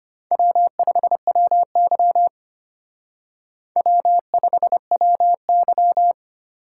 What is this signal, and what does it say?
Morse code, heard as a keyed tone, about 32 wpm, 700 Hz. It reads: W5WY W5WY